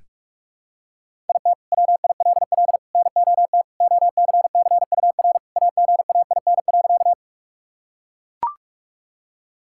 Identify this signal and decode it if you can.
Morse code, heard as a keyed tone, about 42 wpm, 700 Hz. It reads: IT WILL NOT OCCUR AGAIN.
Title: Sentences from Top 1000 Words - Review Part 6 - 45wpm